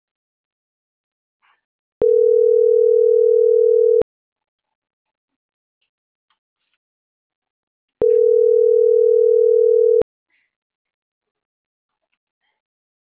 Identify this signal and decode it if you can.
Ringback tone (North America), 440 + 480 Hz.